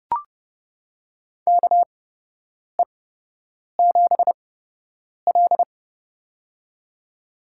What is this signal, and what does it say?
Morse code, heard as a keyed tone, about 30 wpm, spaced as for 8 wpm, 700 Hz. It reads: KE7L
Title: USA Call Signs - 4 Characters - 8x Character Spacing - 30wpm